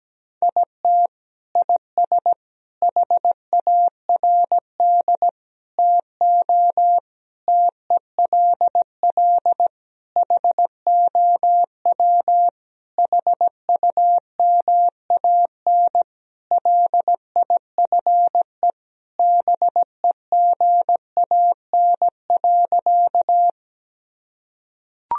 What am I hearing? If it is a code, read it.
Morse code, heard as a keyed tone, about 17 wpm, 700 Hz. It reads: IT IS HARD TO TELL HOW HUMAN LIFE BEGAN.
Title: Sentences from Top 1000 Words - Review Part 3 - 17wpm